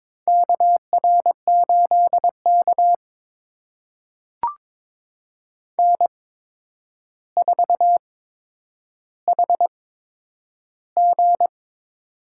Morse code, heard as a keyed tone, about 22 wpm, spaced as for 9 wpm, 700 Hz. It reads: KR8K N4HG
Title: USA Call Signs - 4 Characters - 8x Character Spacing - 22wpm